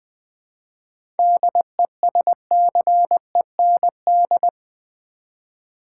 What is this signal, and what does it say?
Morse code, heard as a keyed tone, about 20 wpm, 690 Hz. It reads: DESCEND
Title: 7 Letter Words - First 3 Letters Given - 20wpm